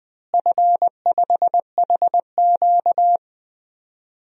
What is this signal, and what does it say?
Morse code, heard as a keyed tone, about 20 wpm, 700 Hz. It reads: F5HQ
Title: DXCC-100 Call Signs - 4 Characters - Repeated 2x - 20wpm